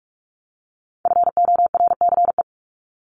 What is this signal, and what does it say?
Morse code, heard as a keyed tone, about 44 wpm, 700 Hz. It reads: FORCE
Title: ED Suffix - Top 100 Words with Base Words 45wpm